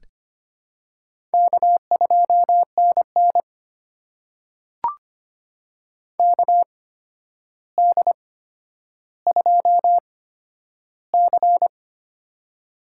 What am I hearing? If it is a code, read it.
Morse code, heard as a keyed tone, about 25 wpm, spaced as for 11 wpm, 700 Hz. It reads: K2NN KD2C
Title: USA Call Signs - 4 Characters - 8x Character Spacing - 25wpm